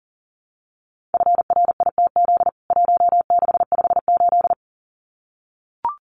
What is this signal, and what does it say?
Morse code, heard as a keyed tone, about 40 wpm, 700 Hz. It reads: FRITZ 1658
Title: CWT Exchange 40wpm